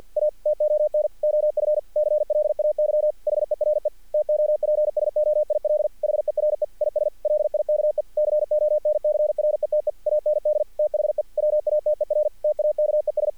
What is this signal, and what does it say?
Morse code, heard as a keyed tone, about 50 wpm, 600 Hz. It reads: A TON OF CLAY HERE TOPSOIL HERE IS LIKE CONCRETE AND THE WATER TAKES
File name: archived / CRSnet